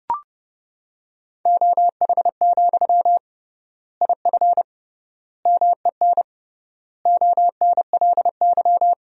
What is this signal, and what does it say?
Morse code, heard as a keyed tone, about 30 wpm, 700 Hz. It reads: OH, IF MEN ONLY
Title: 3x Word-Spacing - Sentences from Top 500 Words - 30wpm